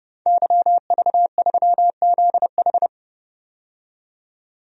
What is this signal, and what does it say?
Morse code, heard as a keyed tone, about 32 wpm, 700 Hz. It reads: YV3ZH
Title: DXCC-100 Call Signs - 5 Characters - 30wpm